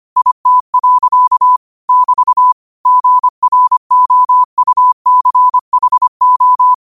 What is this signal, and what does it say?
Morse code, heard as a keyed tone, about 25 wpm, 1000 Hz. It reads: IT. <BT> GROUCHO
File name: qotdfiles / 25wpm